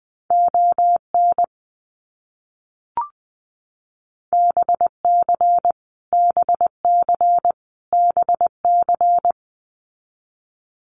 Morse code, heard as a keyed tone, about 19 wpm, 700 Hz. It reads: ON BC BC BC
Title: Canadian Provinces and Territories - Repeated 3x - 20wpm